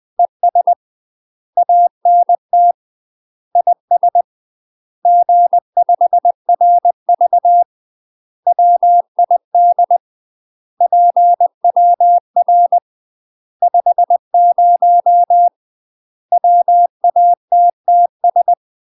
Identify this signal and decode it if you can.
Morse code, heard as a keyed tone, about 20 wpm, 700 Hz. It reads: ES ANT IS G5RV WID PWR 50 WATTS
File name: UK MASTER QSOs TRAINING / 20WPM / UK MASTER QSOs 0044